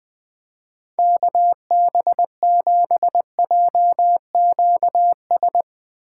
Morse code, heard as a keyed tone, about 22 wpm, 700 Hz. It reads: KB7JQS
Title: Speed Racing - USA Call Signs 20wpm